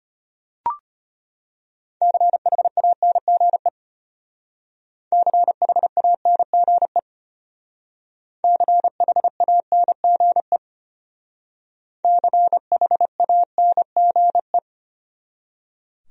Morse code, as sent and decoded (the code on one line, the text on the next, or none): -.-. .... .- -. --. / -.-. .... .- -. --. / -.-. .... .- -. --. / -.-. .... .- -. --. .
CHANGE CHANGE CHANGE CHANGE